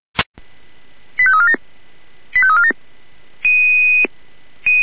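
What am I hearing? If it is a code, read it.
Five-tone selective call (ZVEI), heard as quick runs of five tones, tones of about 70 ms. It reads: 85347 85347
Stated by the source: ZVEI five-tone call 85347 received by SDR.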